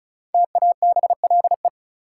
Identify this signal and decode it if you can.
Morse code, heard as a keyed tone, about 36 wpm, 700 Hz. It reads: TABLE
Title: Sentences from Top 400 Words - New Word Drill - 35wpm